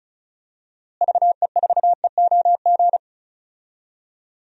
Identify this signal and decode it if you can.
Morse code, heard as a keyed tone, about 36 wpm, 700 Hz. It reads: VE4EOG
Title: DXCC-100 Call Signs - 6 Characters - Repeated 3x - 35wpm